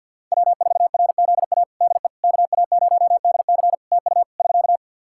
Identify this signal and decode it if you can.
Morse code, heard as a keyed tone, about 50 wpm, 700 Hz. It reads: W4RZA DE KA0DC TU <SK>
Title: Practice QSO - Exchange 4 - Ending - 50wpm